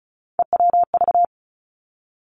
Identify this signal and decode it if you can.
Morse code, heard as a keyed tone, about 36 wpm, 700 Hz. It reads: EWV
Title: Three Challenging Letter-Number Combinations 35wpm